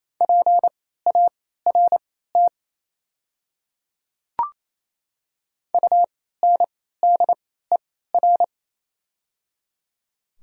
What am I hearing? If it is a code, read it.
Morse code, heard as a keyed tone, about 28 wpm, spaced as for 16 wpm, 700 Hz. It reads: PART UNDER
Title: 3x Character Spacing - Top 300 Words - 28wpm